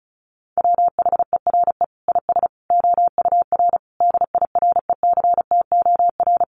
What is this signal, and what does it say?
Morse code, heard as a keyed tone, about 34 wpm, 700 Hz. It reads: WHERE IS OUR DIRECTOR
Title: Sentences from Top 1300 Words 35wpm